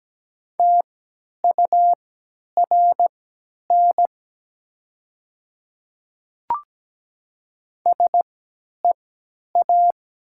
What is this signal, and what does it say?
Morse code, heard as a keyed tone, about 17 wpm, spaced as for 10 wpm, 690 Hz. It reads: TURN SEA
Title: 3x Character Spacing - Top 300 Words - 17wpm